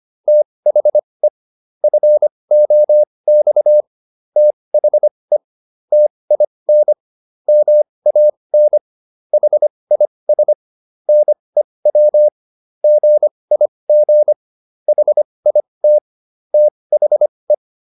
Morse code, heard as a keyed tone, about 25 wpm, spaced as for 20 wpm, 600 Hz. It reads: THE FOX THE TIN MAN HIS NEW GIG HIT THE